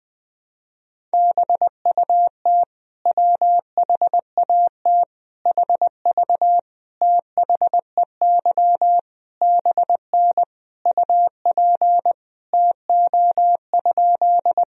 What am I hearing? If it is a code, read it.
Morse code, heard as a keyed tone, about 21 wpm, 700 Hz. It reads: BUT WHAT HV THEY BN UP TO?
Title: Sentences from Top 100 Words and Abbreviations 20wpm